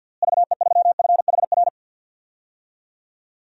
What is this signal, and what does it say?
Morse code, heard as a keyed tone, about 53 wpm, 700 Hz. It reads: VE3FHR
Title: DXCC-100 Call Signs - 6 Characters - Repeated 2x - 50wpm